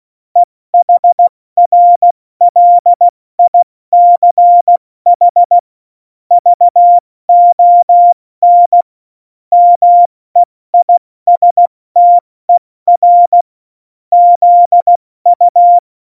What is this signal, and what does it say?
Morse code, heard as a keyed tone, about 16 wpm, 700 Hz. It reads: EHRLICH VON MEISTER ZU
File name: Morsekurs / Witze / Witz 26. Januar 2021, 70 F80